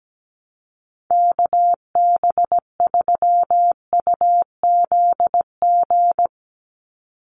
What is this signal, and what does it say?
Morse code, heard as a keyed tone, about 17 wpm, 700 Hz. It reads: KB3UZG
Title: USA Call Signs 17wpm